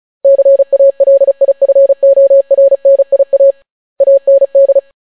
Morse code, heard as a keyed tone, about 36 wpm, 550 Hz. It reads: CALIFORNIA AND